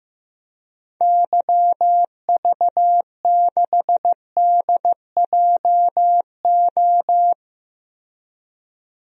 Morse code, heard as a keyed tone, about 15 wpm, 700 Hz. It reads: YV6DJO
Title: DXCC-100 Call Signs - 6 Characters - 15wpm